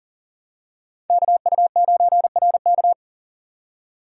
Morse code, heard as a keyed tone, about 40 wpm, 700 Hz. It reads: KU9RK
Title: K1USN SST QSO Practice - 40wpm